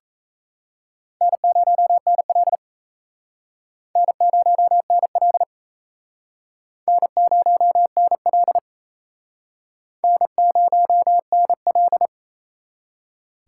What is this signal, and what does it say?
Morse code, heard as a keyed tone, about 35 wpm, 700 Hz. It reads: N0NL N0NL N0NL N0NL